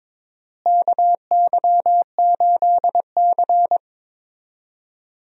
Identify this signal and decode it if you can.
Morse code, heard as a keyed tone, about 22 wpm, 700 Hz. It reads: KY8C
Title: USA Call Signs - 4 Characters - Repeated 3x - 22wpm